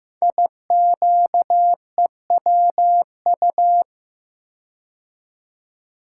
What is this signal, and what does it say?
Morse code, heard as a keyed tone, about 15 wpm, 700 Hz. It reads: IQEWU
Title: Code Groups - 5 Letters - 15wpm